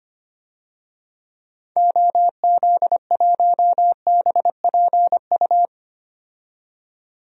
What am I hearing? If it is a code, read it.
Morse code, heard as a keyed tone, about 25 wpm, 700 Hz. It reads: OZ1BPU
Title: DXCC-100 Call Signs - 6 Characters - 25wpm